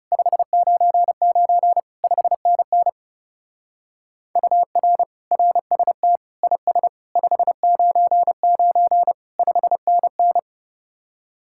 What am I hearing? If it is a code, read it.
Morse code, heard as a keyed tone, about 35 wpm, 700 Hz. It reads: 599 5NN UR RST IS 599 5NN
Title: Speed-Racing Common QSO Phrases - 30wpm